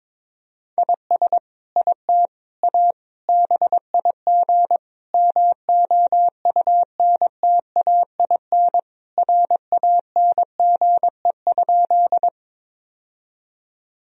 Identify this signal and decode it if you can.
Morse code, heard as a keyed tone, about 22 wpm, 700 Hz. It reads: IS IT A BIG MOUNTAIN RANGE?
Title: Sentences from Top 1000 Words - Review Part 4 - 22wpm